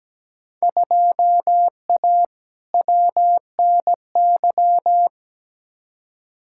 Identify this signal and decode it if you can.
Morse code, heard as a keyed tone, about 17 wpm, 700 Hz. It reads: 2A WNY
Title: Field Day Exchanges - 17wpm